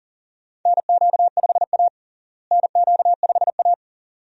Morse code, heard as a keyed tone, about 40 wpm, 700 Hz. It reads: NQ5A NQ5A
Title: USA Call Signs - 4 Characters - Repeated 2x - 40wpm